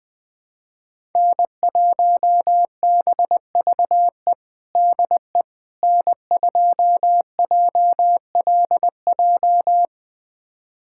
Morse code, heard as a keyed tone, about 21 wpm, 700 Hz. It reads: N1BVE DE N2JLJ
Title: USA Callsign DE Callsign - 5 Characters - 20wpm